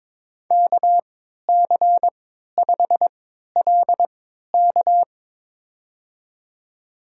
Morse code, heard as a keyed tone, about 22 wpm, spaced as for 12 wpm, 700 Hz. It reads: KC5LK
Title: USA Call Signs - 5 Characters - 3x Character Spacing - 22wpm